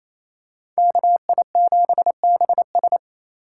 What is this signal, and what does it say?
Morse code, heard as a keyed tone, about 28 wpm, 700 Hz. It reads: KI7BS
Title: USA Call Signs - 5 Characters - 2x Character Spacing - 28wpm